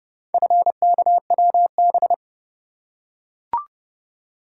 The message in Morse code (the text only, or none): FKWB